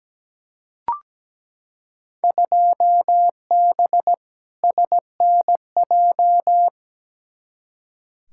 Morse code, heard as a keyed tone, about 17 wpm, 700 Hz. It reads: 2B SNJ
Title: Field Day Exchanges Encore - 17wpm